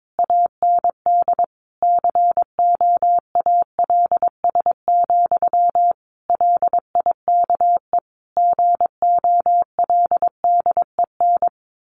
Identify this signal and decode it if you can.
Morse code, heard as a keyed tone, about 22 wpm, 700 Hz. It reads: AND COALS, LIKE GOLDEN
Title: A Sound Of Thunder 22wpm